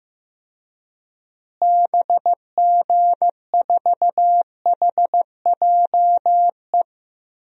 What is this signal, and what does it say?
Morse code, heard as a keyed tone, about 15 wpm, 700 Hz. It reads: BG4HJE
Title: DXCC-100 Call Signs - 6 Characters - Repeated 3x - 15wpm